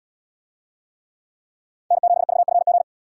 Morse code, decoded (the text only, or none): N7VFG